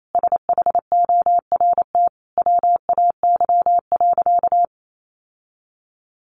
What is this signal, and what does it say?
Morse code, heard as a keyed tone, about 28 wpm, 700 Hz. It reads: SHORT WAY.
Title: Sentences from Top 1400 Words 28wpm